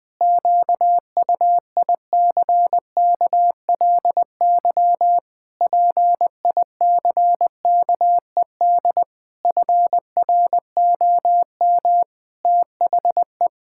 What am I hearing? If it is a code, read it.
Morse code, heard as a keyed tone, about 20 wpm, 700 Hz. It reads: QUICKLY PICKED FROM THE
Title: Pangram Challenge v2 - 20wpm